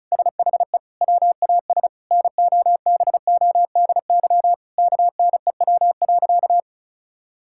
Morse code, heard as a keyed tone, about 34 wpm, 700 Hz. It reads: SHE WAS NOBODY KNEW.
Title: Sentences from Top 1600 Words 35wpm